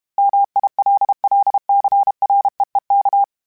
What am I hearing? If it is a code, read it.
Morse code, heard as a keyed tone, about 32 wpm, 800 Hz. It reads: MILLCREEK
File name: fromSMs / FromSMs mp3s / Cities@30